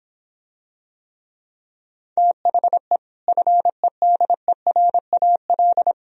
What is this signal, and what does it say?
Morse code, heard as a keyed tone, about 26 wpm, 700 Hz. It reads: THE FEDERAL